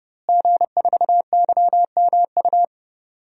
Morse code, heard as a keyed tone, about 30 wpm, 700 Hz. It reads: G4YMU
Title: DXCC-100 Call Signs - 5 Characters - Repeated 2x - 30wpm